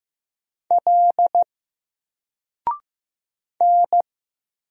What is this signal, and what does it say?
Morse code, heard as a keyed tone, about 15 wpm, 700 Hz. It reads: L N
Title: Single Letter-Number Alt 15wpm